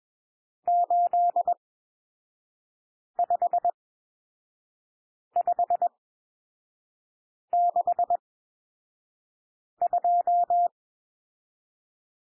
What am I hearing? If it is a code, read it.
Morse code, heard as a keyed tone, about 21 wpm, spaced as for 5 wpm, 700 Hz. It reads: 85562
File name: cwmp3 / N1 SOUND NUMBERS 21 wpm